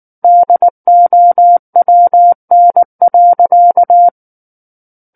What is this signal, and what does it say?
Morse code, heard as a keyed tone, about 19 wpm, 700 Hz. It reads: DOWN.
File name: Book One / 10-Chapter 10 - 19 WPM